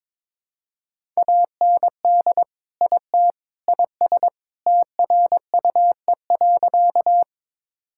Morse code, heard as a keyed tone, about 22 wpm, 700 Hz. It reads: AND IT IS TRUE.
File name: practice / 22wpm / Sentences with 3-Word Phrases - 22wpm